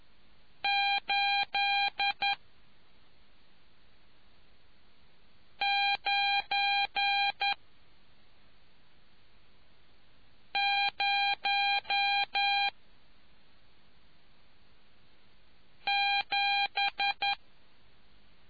Morse code, heard as a keyed tone, about 11 wpm, 800 Hz. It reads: 8 9 0 7